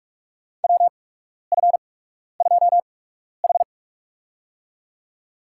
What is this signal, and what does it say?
Morse code, heard as a keyed tone, about 43 wpm, spaced as for 12 wpm, 700 Hz. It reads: WF2H
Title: USA Call Signs - 4 Characters - 8x Character Spacing - 45wpm